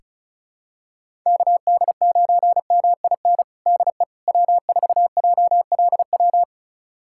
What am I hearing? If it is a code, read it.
Morse code, heard as a keyed tone, about 36 wpm, 700 Hz. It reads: KD9MIN DE W4JLW
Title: USA Callsign DE Callsign 35wpm